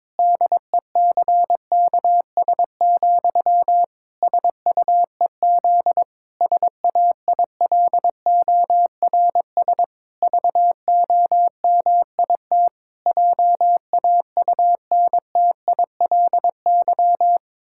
Morse code, heard as a keyed tone, about 22 wpm, 700 Hz. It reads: DECKS, SUEZ SAILORS VOMIT JAUNTILY
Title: Pangram Challenge v2 - 22wpm